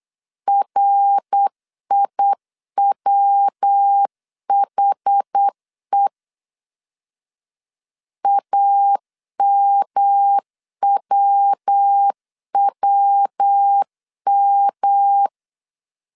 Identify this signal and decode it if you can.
Morse code, heard as a keyed tone, about 8 wpm, 800 Hz. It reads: RIWHE AMWWM
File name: CW / DET / 1 3 1500C S B 8wpm 4141